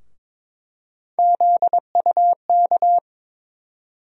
Morse code, heard as a keyed tone, about 21 wpm, 700 Hz. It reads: ZUK
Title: USA Call Sign Postfixes - 3 Letters 22wpm